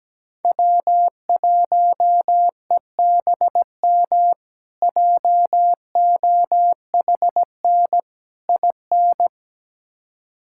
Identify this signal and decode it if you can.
Morse code, heard as a keyed tone, about 17 wpm, 700 Hz. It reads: W1EBM JOHN IN
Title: K1USN SST QSO Practice - 17wpm